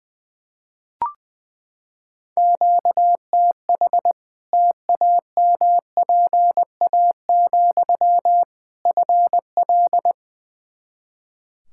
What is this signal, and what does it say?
Morse code, heard as a keyed tone, about 21 wpm, 700 Hz. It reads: QTH TAMPA, FL
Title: Practice QSO - Exchange 1 - Beginning - 20wpm